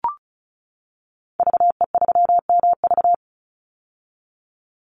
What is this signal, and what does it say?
Morse code, heard as a keyed tone, about 34 wpm, 700 Hz. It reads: VE3MV
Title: Call Signs - 35wpm